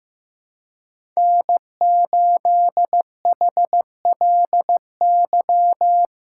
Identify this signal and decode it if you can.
Morse code, heard as a keyed tone, about 15 wpm, 700 Hz. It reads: N8HLY
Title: USA Callsign DE Callsign - 5 Characters - 15wpm